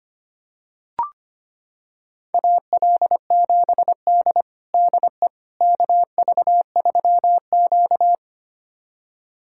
Morse code, heard as a keyed tone, about 25 wpm, 700 Hz. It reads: AL7D DE KV3Q
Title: USA Callsign DE Callsign - 4 Characters - 25wpm